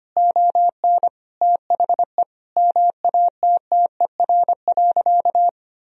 Morse code, heard as a keyed tone, about 25 wpm, 700 Hz. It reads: ON THE MATTER.
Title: Sentences from Top 1000 Words - Review Part 4 - 25wpm